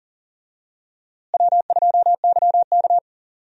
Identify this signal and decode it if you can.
Morse code, heard as a keyed tone, about 38 wpm, 700 Hz. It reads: W2YK
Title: USA Call Signs - 4 Characters - Repeated 2x - 40wpm